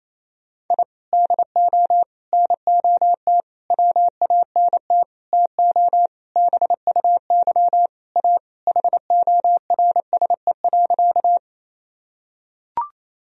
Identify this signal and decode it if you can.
Morse code, heard as a keyed tone, about 28 wpm, 700 Hz. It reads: I DO NOT WANT TO BUY A HORSE.
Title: Sentences from Top 1000 Words - Review Part 4 - 28wpm